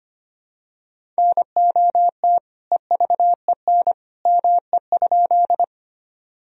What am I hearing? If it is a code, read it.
Morse code, heard as a keyed tone, about 25 wpm, 700 Hz. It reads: NOT EVEN ME?
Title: Sentences from Top 1000 Words - Review Part 1 - 25wpm